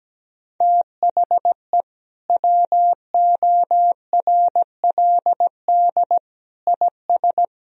Morse code, heard as a keyed tone, about 17 wpm, 700 Hz. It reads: THE WORLD IS
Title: Sentences from Top 200 Words and Abbreviations 17wpm